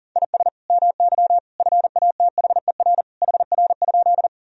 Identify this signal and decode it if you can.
Morse code, heard as a keyed tone, about 40 wpm, 700 Hz. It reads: IS MY FATHER HR?
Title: Sentences from Top 300 Words and Abbreviations 40wpm